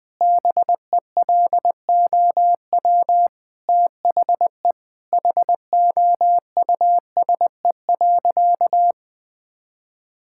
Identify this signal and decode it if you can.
Morse code, heard as a keyed tone, about 21 wpm, 700 Hz. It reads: BELOW THE HOUSE.
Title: Sentences from Top 1000 Words - Review Part 5 - 20wpm